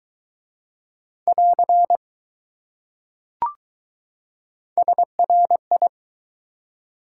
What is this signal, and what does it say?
Morse code, heard as a keyed tone, about 20 wpm, 700 Hz. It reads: <AR> SRI